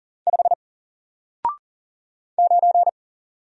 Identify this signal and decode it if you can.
Morse code, heard as a keyed tone, about 40 wpm, 700 Hz. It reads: 5 9
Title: Single Character (Letter:Number:Punctuation:Prosign) Alt 40wpm